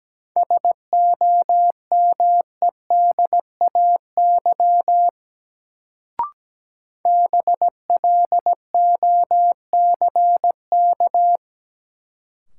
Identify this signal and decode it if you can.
Morse code, heard as a keyed tone, about 17 wpm, 700 Hz. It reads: SOMEDAY BLOCK
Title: Sentences from Top 1600 Words - New Word Drill Alt - 17wpm